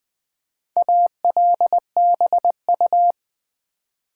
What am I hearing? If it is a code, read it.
Morse code, heard as a keyed tone, about 20 wpm, 700 Hz. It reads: ALBU
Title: Build A Word - 1 Letter At A Time - 5-Letter Words - 20wpm